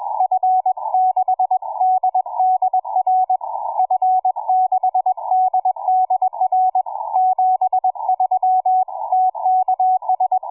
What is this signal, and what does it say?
Morse code, heard as a keyed tone, about 21 wpm, 740 Hz, with radio noise. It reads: F6DDR F6DDR 73 TKS
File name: cw forum / pj4 k8gg 1827 160m 05h26